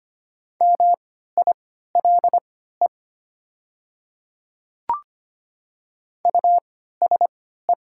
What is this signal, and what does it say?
Morse code, heard as a keyed tone, about 25 wpm, spaced as for 14 wpm, 690 Hz. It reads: MILE USE
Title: 3x Character Spacing - Top 300 Words - 25wpm